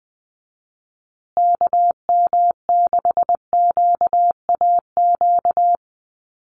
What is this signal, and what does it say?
Morse code, heard as a keyed tone, about 20 wpm, 700 Hz. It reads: KM6QAQ
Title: USA Call Signs Repeated 3x 20wpm